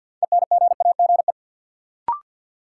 Morse code, heard as a keyed tone, about 52 wpm, 700 Hz. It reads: ENGAGE
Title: Sentences from Top 1500 Words - New Word Drill - 50wpm